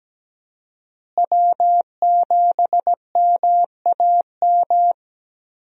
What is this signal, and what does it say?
Morse code, heard as a keyed tone, about 17 wpm, 700 Hz. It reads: W7MAM
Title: USA Call Signs - 5 Characters - Repeated 3x - 17wpm